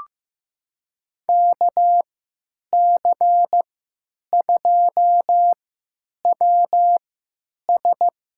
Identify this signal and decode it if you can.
Morse code, heard as a keyed tone, about 15 wpm, spaced as for 9 wpm, 700 Hz. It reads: KC2WS
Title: USA Call Signs - 3x Character Spacing - 15wpm